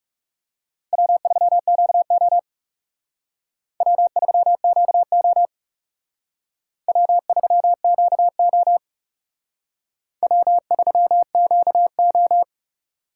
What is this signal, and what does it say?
Morse code, heard as a keyed tone, about 36 wpm, 700 Hz. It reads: W3QO W3QO W3QO W3QO
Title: Speed Racing - USA Call Signs 30wpm